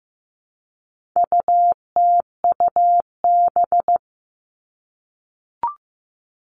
Morse code, heard as a keyed tone, about 15 wpm, 700 Hz. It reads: UTUB
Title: All Cut Numbers - 4-Digit 15wpm